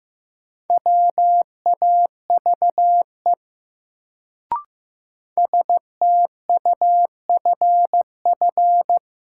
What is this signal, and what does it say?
Morse code, heard as a keyed tone, about 15 wpm, 700 Hz. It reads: WAVE STUFF
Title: Sentences from Top 1400 Words - New Word Drill Alt - 15wpm